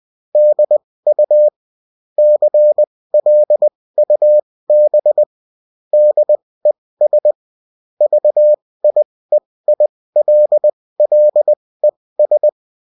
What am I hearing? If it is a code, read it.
Morse code, heard as a keyed tone, about 20 wpm, spaced as for 16 wpm, 600 Hz. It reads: DU CLUB DES VIEILLES
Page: bulletin-uft-sous-forme-texte-et-en-cw